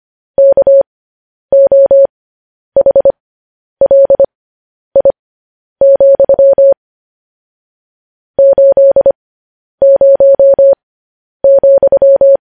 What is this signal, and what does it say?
Morse code, heard as a keyed tone, about 25 wpm, spaced as for 10 wpm, 550 Hz. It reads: KOHLI, 80,
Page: news headlines in morse code at 10 wpm